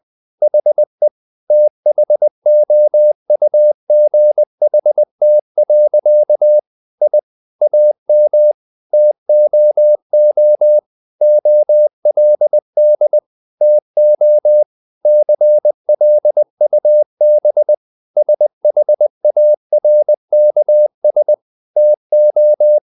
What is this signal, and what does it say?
Morse code, heard as a keyed tone, about 20 wpm, 600 Hz. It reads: HE THOUGHT. I AM TOO OLD TO CLUB SHARKS TO
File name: the old man and the sea part 2 20wpm